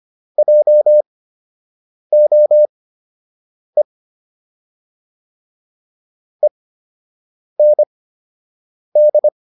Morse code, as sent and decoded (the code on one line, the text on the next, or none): .--- --- . / . -. -..
JOE END